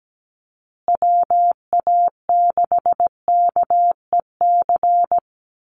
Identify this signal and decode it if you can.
Morse code, heard as a keyed tone, about 17 wpm, 700 Hz. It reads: WA6KEC
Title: USA Call Signs Repeated 2x 17wpm